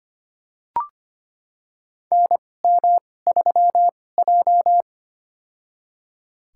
Morse code, heard as a keyed tone, about 25 wpm, spaced as for 18 wpm, 700 Hz. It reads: NM3J
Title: USA Call Signs - 4 Characters - 2x Character Spacing - 25wpm